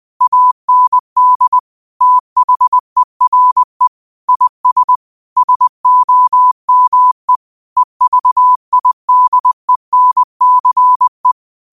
Morse code, heard as a keyed tone, about 20 wpm, 1000 Hz. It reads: AND THERE IS SOME EVIDENCE
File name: qotdfiles / 20wpm